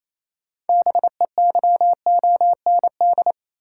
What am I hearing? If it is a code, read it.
Morse code, heard as a keyed tone, about 28 wpm, 700 Hz. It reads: BEYOND
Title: Sentences from Top 1100 Words - New Word Drill Alt - 28wpm